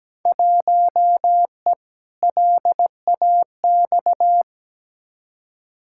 Morse code, heard as a keyed tone, about 17 wpm, 700 Hz. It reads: 1E LAX
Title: Field Day Exchanges - 17wpm